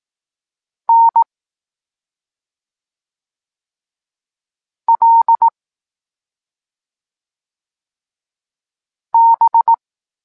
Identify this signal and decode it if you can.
Morse code, heard as a keyed tone, about 18 wpm, 920 Hz. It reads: N L B
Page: Tekst 23.3 – Ciąg dalszy doskonalenia odbioru poznanych liter